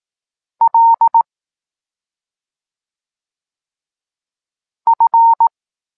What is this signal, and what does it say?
Morse code, heard as a keyed tone, about 18 wpm, 920 Hz. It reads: L F
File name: Tekst-26 1